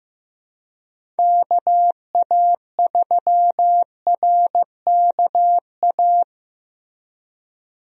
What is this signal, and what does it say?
Morse code, heard as a keyed tone, about 15 wpm, 700 Hz. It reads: KA3RKA
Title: USA Call Signs - 6 Characters - Repeated 2x - 15wpm